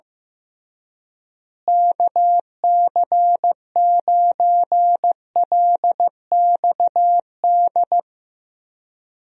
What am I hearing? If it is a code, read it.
Morse code, heard as a keyed tone, about 15 wpm, 700 Hz. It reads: KC9LXD
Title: USA Call Signs - 6 Characters - Repeated 3x - 15wpm